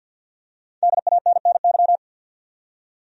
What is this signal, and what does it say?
Morse code, heard as a keyed tone, about 49 wpm, 700 Hz. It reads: DANNY
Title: USA Common Names - Max 5 Letters - 50wpm